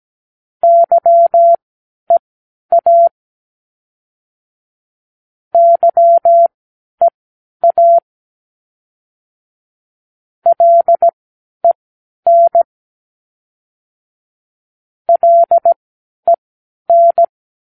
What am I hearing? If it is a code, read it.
Morse code, heard as a keyed tone, about 17 wpm, spaced as for 10 wpm, 680 Hz. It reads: YEA YEA LEN LEN
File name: k7qo / Practicewords